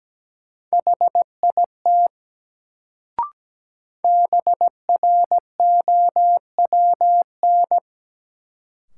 Morse code, heard as a keyed tone, about 17 wpm, 700 Hz. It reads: HIT BROWN
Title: Sentences from Top 1200 Words - New Word Drill Alt - 17wpm